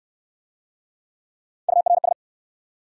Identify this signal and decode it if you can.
Morse code, heard as a keyed tone, about 69 wpm, 700 Hz. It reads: URS